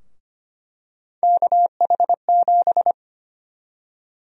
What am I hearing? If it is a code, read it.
Morse code, heard as a keyed tone, about 25 wpm, 700 Hz. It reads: KH7
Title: USA Call Sign Prefixes - Repeated 2x - 25wpm